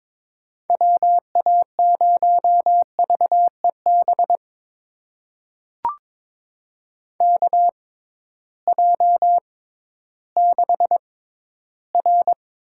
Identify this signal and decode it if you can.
Morse code, heard as a keyed tone, about 22 wpm, spaced as for 13 wpm, 700 Hz. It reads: WA0VEB KJ6R
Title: USA Call Signs - 6 Characters - 6x Character Spacing - 22wpm